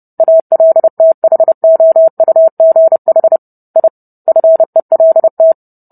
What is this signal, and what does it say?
Morse code, heard as a keyed tone, about 30 wpm, 650 Hz. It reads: ALTHOUGH I FELT